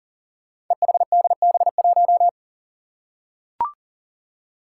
Morse code, heard as a keyed tone, about 40 wpm, 700 Hz. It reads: EHDB1
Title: Five Challenging Letter-Number Combinations 40wpm